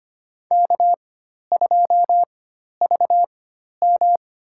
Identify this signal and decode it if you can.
Morse code, heard as a keyed tone, about 25 wpm, spaced as for 12 wpm, 700 Hz. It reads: K2VM
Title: USA Call Signs - 4 Characters - 4x Character Spacing - 25wpm